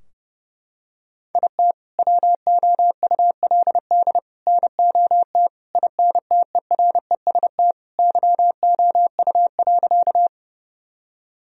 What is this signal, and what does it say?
Morse code, heard as a keyed tone, about 30 wpm, 700 Hz. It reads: IT WOULD NOT INTEREST YOU.
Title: Sentences from Top 1000 Words - Review Part 2 - 30wpm